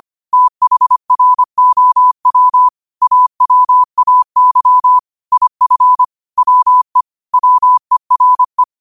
Morse code, heard as a keyed tone, about 25 wpm, 1000 Hz. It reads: THROW AWAY IF WE WERE